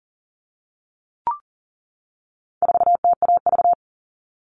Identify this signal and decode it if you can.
Morse code, heard as a keyed tone, about 40 wpm, 700 Hz. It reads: 4TAV
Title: All Cut Numbers - 4-Digit 40wpm